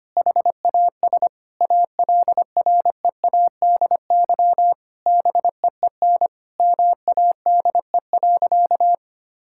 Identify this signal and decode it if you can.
Morse code, heard as a keyed tone, about 25 wpm, 700 Hz. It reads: HAS ALREADY BEEN MADE.
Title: Sentences from Top 1000 Words - Review Part 5 - 25wpm